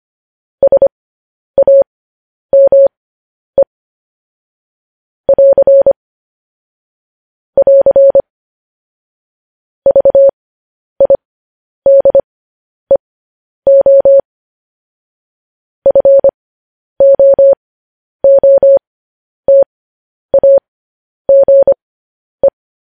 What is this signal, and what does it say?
Morse code, heard as a keyed tone, about 25 wpm, spaced as for 10 wpm, 550 Hz. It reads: SAME <AR> <AR> VIDEO FOOTAGE
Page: news headlines in morse code at 10 wpm